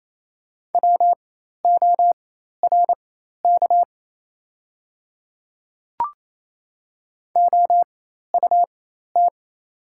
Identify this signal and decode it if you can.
Morse code, heard as a keyed tone, about 27 wpm, spaced as for 13 wpm, 690 Hz. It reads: WORK OUT
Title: 4x Character Spacing - Top 300 Words - 28wpm